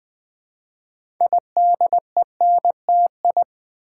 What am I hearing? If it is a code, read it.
Morse code, heard as a keyed tone, about 20 wpm, 700 Hz. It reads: IDENTI
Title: Build A Word - 2 Letters At A Time - 8-Letter Words - 20wpm